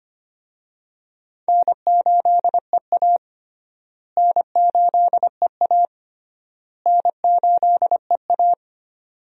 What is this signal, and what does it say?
Morse code, heard as a keyed tone, about 25 wpm, 700 Hz. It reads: N8EA N8EA N8EA
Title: USA Call Signs - 4 Characters - Repeated 3x - 25wpm